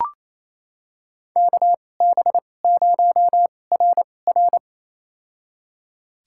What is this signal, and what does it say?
Morse code, heard as a keyed tone, about 28 wpm, spaced as for 20 wpm, 700 Hz. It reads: KB0RR
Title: USA Call Signs - 2x Character Spacing - 28wpm